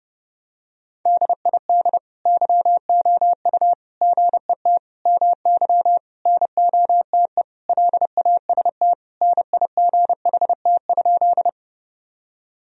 Morse code, heard as a keyed tone, about 30 wpm, 700 Hz. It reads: DID YOU GET MY NOTE LAST NIGHT?
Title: Sentences from Top 1000 Words - Review Part 2 - 30wpm